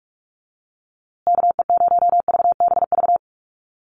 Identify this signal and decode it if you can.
Morse code, heard as a keyed tone, about 44 wpm, 700 Hz. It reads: KE0VBV
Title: USA Call Signs Repeated 3x 45wpm